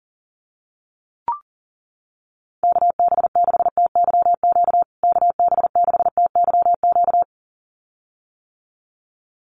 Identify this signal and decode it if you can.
Morse code, heard as a keyed tone, about 40 wpm, 700 Hz. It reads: KB6TYQ KB6TYQ
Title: USA Call Signs Repeated 2x 40wpm